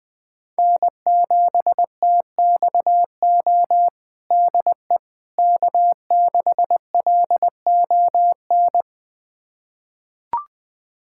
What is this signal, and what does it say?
Morse code, heard as a keyed tone, about 20 wpm, 700 Hz. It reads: N7TXO DE K6LON
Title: USA Callsign DE Callsign - 5 Characters - 20wpm